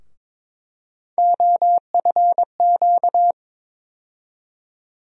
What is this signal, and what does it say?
Morse code, heard as a keyed tone, about 22 wpm, 700 Hz. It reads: OFQ
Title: USA Call Sign Postfixes - 3 Letters 22wpm